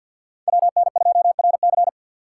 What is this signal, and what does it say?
Morse code, heard as a keyed tone, about 52 wpm, 700 Hz. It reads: WN2RC
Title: Speed Racing - USA 2x2 Call Signs 50wpm